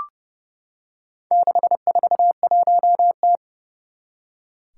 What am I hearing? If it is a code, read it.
Morse code, heard as a keyed tone, about 31 wpm, 700 Hz. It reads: 641T
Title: Four Challenging Letter-Number Combinations 30wpm